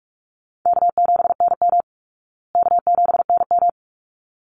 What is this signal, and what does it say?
Morse code, heard as a keyed tone, about 44 wpm, 700 Hz. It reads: K7NM K7NM
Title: Easy Call Signs Repeated 2x - 45wpm